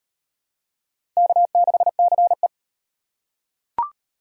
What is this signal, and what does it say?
Morse code, heard as a keyed tone, about 38 wpm, 700 Hz. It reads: K6CE